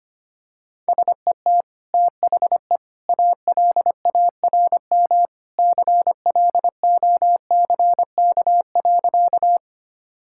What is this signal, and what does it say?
Morse code, heard as a keyed tone, about 25 wpm, 700 Hz. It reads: SET THE ALARM CLOCK.
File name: Easy Sentences with Hints - 25wpm